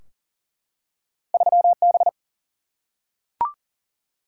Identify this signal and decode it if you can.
Morse code, heard as a keyed tone, about 40 wpm, 700 Hz. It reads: 3B
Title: Single-Letter Single-Number 40wpm